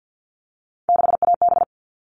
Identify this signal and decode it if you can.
Morse code, heard as a keyed tone, about 48 wpm, 700 Hz. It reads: 6AB
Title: All Cut Numbers - 3-Digit 50wpm